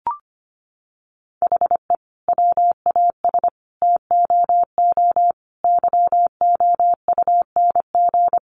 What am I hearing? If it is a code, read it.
Morse code, heard as a keyed tone, about 25 wpm, 700 Hz. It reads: HE WAS TOO YOUNG
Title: Sentences from Top 600 Words 25wpm